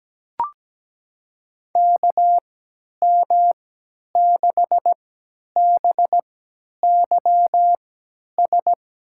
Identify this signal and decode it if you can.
Morse code, heard as a keyed tone, about 17 wpm, spaced as for 10 wpm, 700 Hz. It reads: KM6BYS
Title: USA Call Signs - 6 Characters - 3x Character Spacing - 17wpm